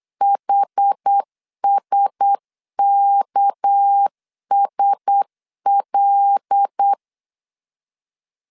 Morse code, heard as a keyed tone, about 8 wpm, 800 Hz. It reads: HSKSL
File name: LEZIONI / 7 GF9Z / 7 GF9Z 8wpm 3213